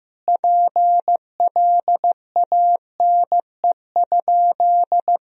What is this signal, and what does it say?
Morse code, heard as a keyed tone, about 15 wpm, 700 Hz. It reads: PLANE?
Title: Sentences from Top 1000 Words - Review Part 3 - 15wpm